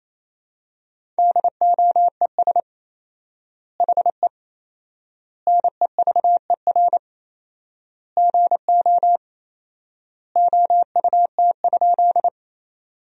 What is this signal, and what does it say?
Morse code, heard as a keyed tone, about 27 wpm, 700 Hz. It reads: DOES HE NEVER GO OUT?
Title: 4x Word-Spacing - Sentences from Top 300 Words - 28wpm